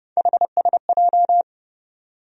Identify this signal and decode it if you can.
Morse code, heard as a keyed tone, about 30 wpm, 700 Hz. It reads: HSJ
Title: USA Call Sign Postfixes - 3 Letters 30wpm